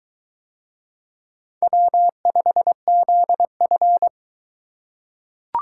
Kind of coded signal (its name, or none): Morse code